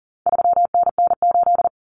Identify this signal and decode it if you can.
Morse code, heard as a keyed tone, about 40 wpm, 700 Hz. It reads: 3NN8
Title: Common Cut Numbers - 4-Digit 40wpm